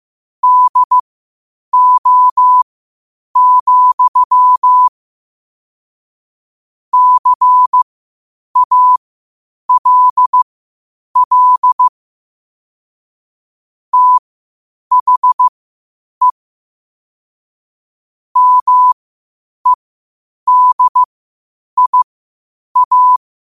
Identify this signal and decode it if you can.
Morse code, heard as a keyed tone, about 15 wpm, spaced as for 8 wpm, 1000 Hz. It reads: DO, CALL THE MEDIA
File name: qotdfiles / 8wpm